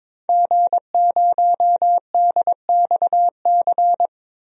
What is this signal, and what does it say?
Morse code, heard as a keyed tone, about 22 wpm, 700 Hz. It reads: G0DXC
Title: DXCC-100 Call Signs - 5 Characters - 22wpm